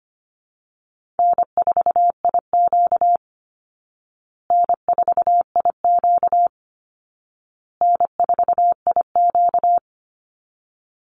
Morse code, heard as a keyed tone, about 25 wpm, 700 Hz. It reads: N4IQ N4IQ N4IQ
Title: Easy Call Signs Repeated 3x - 25wpm